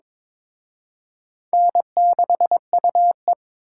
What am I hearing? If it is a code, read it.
Morse code, heard as a keyed tone, about 26 wpm, 700 Hz. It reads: N6UE
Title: Speed Racing - USA 1x2 Call Signs 22wpm